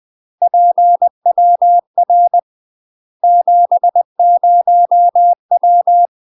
Morse code, heard as a keyed tone, about 20 wpm, 700 Hz. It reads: PWR 70W
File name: UK MASTER QSOs TRAINING / 20WPM / UK MASTER QSOs 0060